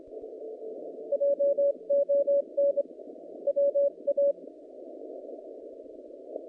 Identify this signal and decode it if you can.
Morse code, heard as a keyed tone, about 24 wpm, 560 Hz. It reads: JON WA
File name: contests / naqp-cw-k5tr-op-k5pi- -0000